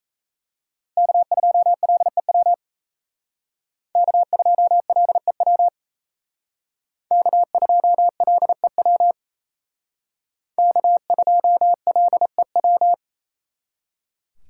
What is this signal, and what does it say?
Morse code, heard as a keyed tone, about 36 wpm, 700 Hz. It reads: K2LEW K2LEW K2LEW K2LEW